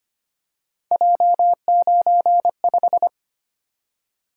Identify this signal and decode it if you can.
Morse code, heard as a keyed tone, about 25 wpm, 700 Hz. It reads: J95
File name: Three Letter-Number Combinations - 25wpm